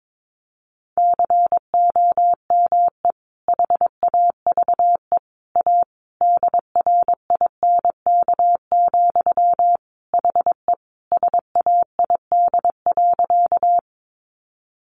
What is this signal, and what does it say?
Morse code, heard as a keyed tone, about 22 wpm, 700 Hz. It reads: COME HAVE A DRINK, HE SAID.
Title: Sentences from Top 1500 Words 22wpm